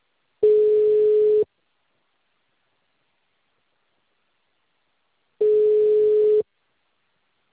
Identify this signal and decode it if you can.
Ringback tone (Europe, 425 Hz).